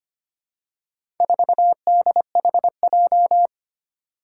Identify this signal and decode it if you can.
Morse code, heard as a keyed tone, about 25 wpm, 700 Hz. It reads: 4DHJ
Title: Four Challenging Letter-Number Combinations 25wpm